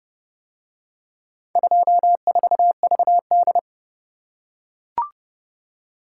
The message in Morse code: 24VD

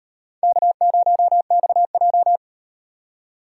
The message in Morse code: K0XJ

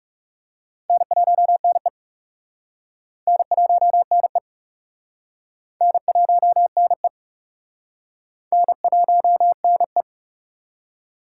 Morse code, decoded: N1NE N1NE N1NE N1NE

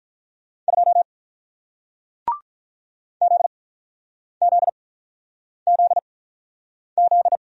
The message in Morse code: 2 ZZZZ

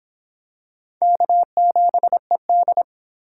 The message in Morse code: K7ED

